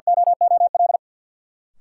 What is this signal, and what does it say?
Morse code, heard as a keyed tone, about 52 wpm, 700 Hz. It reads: OOL
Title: Sentences from Top 300 Words - New Word Drill - 50wpm